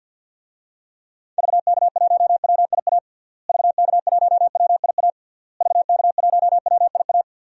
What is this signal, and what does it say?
Morse code, heard as a keyed tone, about 53 wpm, 700 Hz. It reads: VK1WIA VK1WIA VK1WIA
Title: DXCC-100 Call Signs - 6 Characters - Repeated 3x - 50wpm